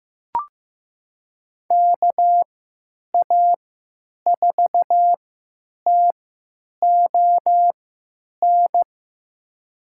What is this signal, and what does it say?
Morse code, heard as a keyed tone, about 15 wpm, spaced as for 9 wpm, 700 Hz. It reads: KA4TON
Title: USA Call Signs - 3x Character Spacing - 15wpm